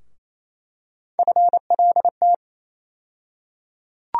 Morse code, heard as a keyed tone, about 28 wpm, 700 Hz. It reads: FLT
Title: USA Call Sign Postfixes - 3 Letters 28wpm